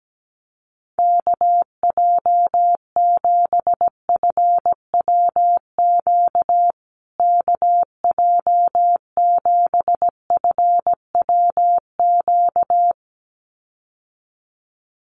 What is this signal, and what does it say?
Morse code, heard as a keyed tone, about 17 wpm, 700 Hz. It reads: KJ7FWQ KJ7FWQ
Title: USA Call Signs Repeated 2x 17wpm